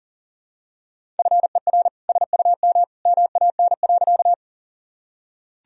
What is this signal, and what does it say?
Morse code, heard as a keyed tone, about 37 wpm, 700 Hz. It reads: FER SUM MAN.
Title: Sentences from Top 200 Words and Abbreviations 40wpm